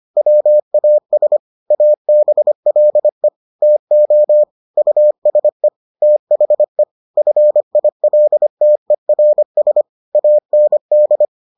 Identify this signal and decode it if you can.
Morse code, heard as a keyed tone, about 25 wpm, 600 Hz. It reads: WAS ABLE TO USE THE FILTERS AND